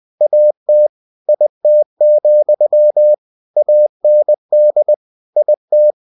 Morse code, heard as a keyed tone, about 20 wpm, 600 Hz. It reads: AT IT, AND IT
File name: cw-books / winnie-the-pooh / winnie-20wpm-page-0037